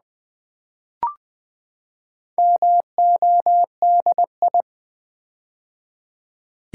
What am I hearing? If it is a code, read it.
Morse code, heard as a keyed tone, about 20 wpm, 700 Hz. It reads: MODI
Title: Build A Word - 4 Letters At A Time - 12-Letter Words - 20wpm